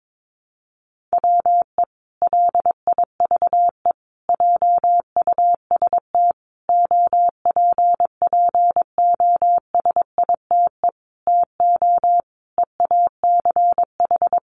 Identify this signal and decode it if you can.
Morse code, heard as a keyed tone, about 22 wpm, 700 Hz. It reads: WE LIVE JUST OPPOSITE TO EACH
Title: Sentences from Top 1300 Words 22wpm